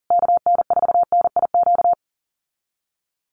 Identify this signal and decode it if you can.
Morse code, heard as a keyed tone, about 40 wpm, 700 Hz. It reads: KN4NIQ
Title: USA Call Signs Repeated 3x 40wpm